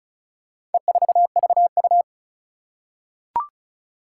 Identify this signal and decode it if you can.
Morse code, heard as a keyed tone, about 36 wpm, 700 Hz. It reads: E4VU
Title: Four Challenging Letter-Number Combinations 35wpm